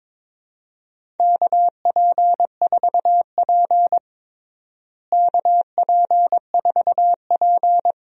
Morse code, heard as a keyed tone, about 22 wpm, 700 Hz. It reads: KP4P KP4P
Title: DXCC-100 Call Signs - 4 Characters - Repeated 3x - 22wpm